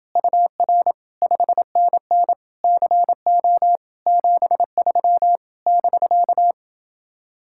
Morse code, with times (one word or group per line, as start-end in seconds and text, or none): 0.2-0.9 UR
1.2-2.3 5NN
2.6-3.8 CO
4.1-5.4 73
5.7-6.5 <BK>